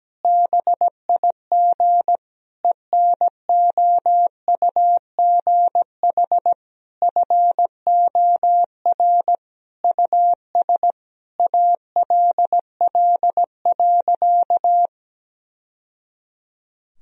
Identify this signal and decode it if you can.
Morse code, heard as a keyed tone, about 17 wpm, 700 Hz. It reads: BIG ENOUGH FOR US ALL.
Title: Sentences from Top 1000 Words - Review Part 1 - 17wpm